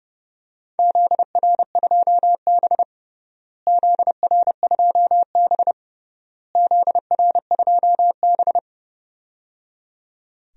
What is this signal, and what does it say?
Morse code, heard as a keyed tone, about 30 wpm, 700 Hz. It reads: ZR2B ZR2B ZR2B
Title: DXCC-100 Call Signs - 4 Characters - Repeated 3x - 30wpm